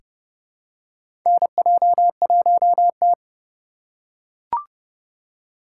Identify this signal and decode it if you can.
Morse code, heard as a keyed tone, about 30 wpm, 700 Hz. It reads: NJ1T
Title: K1USN SST QSO Practice - 30wpm